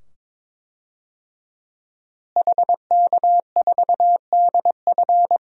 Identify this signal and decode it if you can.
Morse code, heard as a keyed tone, about 22 wpm, 700 Hz. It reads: HK4DF
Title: DXCC-100 Call Signs - 5 Characters - Repeated 2x - 22wpm